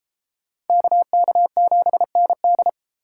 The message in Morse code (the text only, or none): KK7ND